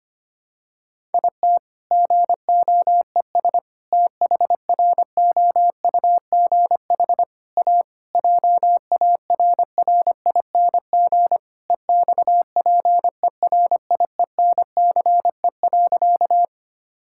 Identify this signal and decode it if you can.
Morse code, heard as a keyed tone, about 25 wpm, 700 Hz. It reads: IT GOES THROUGH A JARRING EXPERIENCE.
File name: practice / 25wpm / All-Age Appropriate Jokes - 25wpm